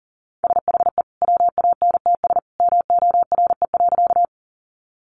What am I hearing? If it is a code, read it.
Morse code, heard as a keyed tone, about 40 wpm, 700 Hz. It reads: SHE WANTS MORE.
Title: Easy Phrases 40wpm